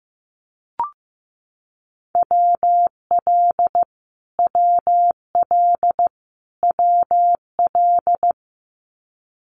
Morse code, heard as a keyed tone, about 15 wpm, 700 Hz. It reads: WL WL WL
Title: Abbreviations Repeated 3x - 15wpm